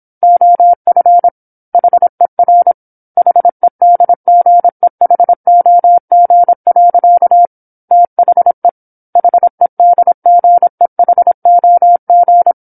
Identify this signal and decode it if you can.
Morse code, heard as a keyed tone, about 27 wpm, 700 Hz. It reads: OF HER HEDGEHOG. THE HEDGEHOG
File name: 08-Chapter-08 26 WPM